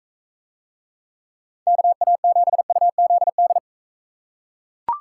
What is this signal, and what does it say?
Morse code, heard as a keyed tone, about 41 wpm, 700 Hz. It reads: KA7UZD